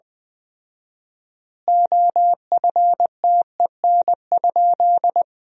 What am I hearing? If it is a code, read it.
Morse code, heard as a keyed tone, about 20 wpm, 700 Hz. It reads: OFTEN?
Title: 4x Word-Spacing - Sentences from Top 400 Words - 20wpm